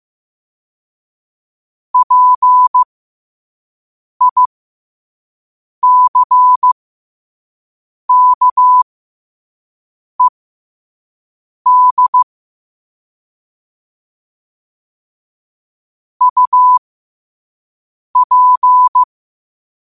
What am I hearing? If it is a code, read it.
Morse code, heard as a keyed tone, about 15 wpm, spaced as for 5 wpm, 1000 Hz. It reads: PICKED UP
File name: qotdfiles / 5wpm